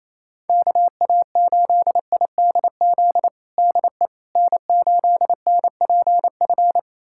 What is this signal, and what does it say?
Morse code, heard as a keyed tone, about 28 wpm, 700 Hz. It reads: KA8IDZ DE N8NPF
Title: USA Callsign DE Callsign 28wpm